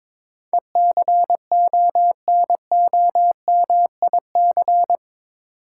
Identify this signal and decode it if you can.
Morse code, heard as a keyed tone, about 22 wpm, 700 Hz. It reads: ECONOMIC
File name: practice / 22wpm / Eight Letter Words - 22wpm